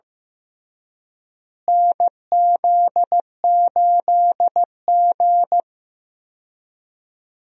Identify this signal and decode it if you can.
Morse code, heard as a keyed tone, about 15 wpm, 700 Hz. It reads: NZ8G
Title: USA Call Signs - 4 Characters - Repeated 3x - 15wpm